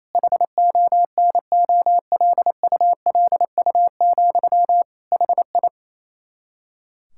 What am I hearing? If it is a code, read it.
Morse code, heard as a keyed tone, about 27 wpm, 700 Hz. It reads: HONOLULU, HI
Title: Practice QSO - Exchange 1 - Beginning - 28wpm